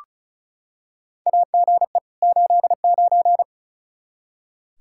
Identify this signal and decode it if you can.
Morse code, heard as a keyed tone, about 34 wpm, 700 Hz. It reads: AGE 89
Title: K1USN SST QSO Practice 35wpm